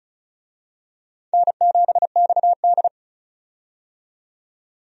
Morse code, heard as a keyed tone, about 34 wpm, 700 Hz. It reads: N7XD
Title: USA Call Signs - 4 Characters - 35wpm